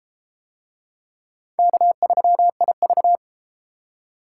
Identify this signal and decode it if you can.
Morse code, heard as a keyed tone, about 30 wpm, 700 Hz. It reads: K3IV